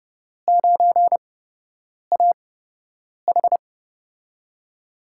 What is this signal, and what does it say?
Morse code, heard as a keyed tone, about 29 wpm, spaced as for 8 wpm, 700 Hz. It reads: 9AH
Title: USA Call Signs - 5 Characters - 8x Character Spacing - 30wpm